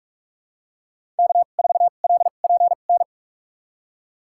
Morse code, heard as a keyed tone, about 46 wpm, spaced as for 32 wpm, 700 Hz. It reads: K4LPN
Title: USA Call Signs - 2x Character Spacing - 45wpm